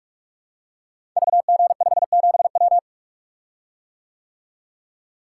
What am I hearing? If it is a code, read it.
Morse code, heard as a keyed tone, about 44 wpm, 700 Hz. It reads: VG57W
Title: Code Groups - 5 Letter:Numbers - 45wpm